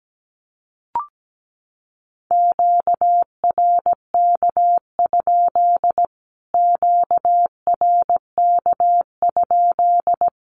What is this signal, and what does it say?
Morse code, heard as a keyed tone, about 17 wpm, 700 Hz. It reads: QRK? QRK?
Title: Q-Codes Repeated 3x 17wpm